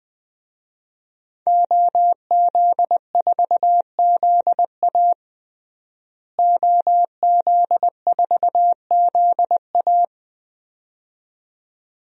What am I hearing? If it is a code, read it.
Morse code, heard as a keyed tone, about 20 wpm, 700 Hz. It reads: OZ4ZA OZ4ZA
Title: DXCC-100 Call Signs - 5 Characters - Repeated 2x - 20wpm